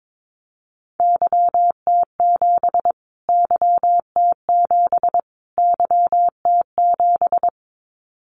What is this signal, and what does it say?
Morse code, heard as a keyed tone, about 22 wpm, 700 Hz. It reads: YT7 YT7 YT7
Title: Callsign Prefixes for Top 25 DXCC - Repeated 3x - 22wpm